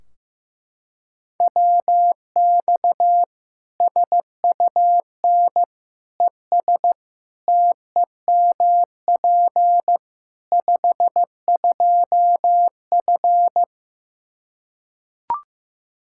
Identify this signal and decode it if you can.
Morse code, heard as a keyed tone, about 15 wpm, 700 Hz. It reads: WX SUN ES TEMP 52F
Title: Practice QSO - Exchange 2 - Info - 15wpm